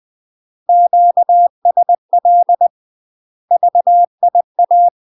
Morse code, heard as a keyed tone, about 20 wpm, 700 Hz. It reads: QSL VIA
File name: MASTER QSOs TRAINING/20WPM